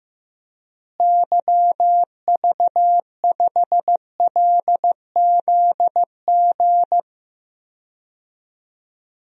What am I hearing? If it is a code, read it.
Morse code, heard as a keyed tone, about 15 wpm, 700 Hz. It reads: YV5LZG
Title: DXCC-100 Call Signs - 6 Characters - Repeated 3x - 15wpm